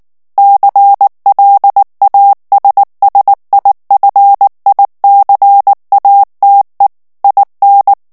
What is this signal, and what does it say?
Morse code, heard as a keyed tone, about 19 wpm, 800 Hz. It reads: CLASSIFICATE IN
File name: morse mp3 / ITALIEN texte clair 2 vitesse 18 mots minute